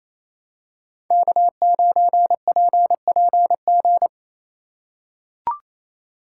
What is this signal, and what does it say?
Morse code, heard as a keyed tone, about 28 wpm, 700 Hz. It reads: K9PPG
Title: USA Call Signs - 3x Character Spacing - 28wpm